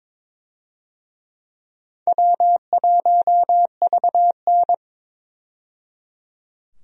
Morse code, heard as a keyed tone, about 23 wpm, 700 Hz. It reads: W1VN
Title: USA Call Signs - 4 Characters - 22wpm